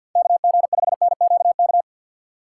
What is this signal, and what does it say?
Morse code, heard as a keyed tone, about 53 wpm, 700 Hz. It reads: KG5NQK
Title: USA Call Signs - 6 Characters - 4x Character Spacing - 50wpm